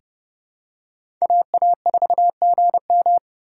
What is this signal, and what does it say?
Morse code, heard as a keyed tone, about 30 wpm, 700 Hz. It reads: AA4GM
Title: USA Call Signs - 5 Characters - 8x Character Spacing - 30wpm